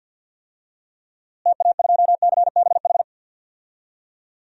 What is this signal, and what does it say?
Morse code, heard as a keyed tone, about 51 wpm, 700 Hz. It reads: TA2CBH
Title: DXCC-100 Call Signs - 6 Characters - Repeated 2x - 50wpm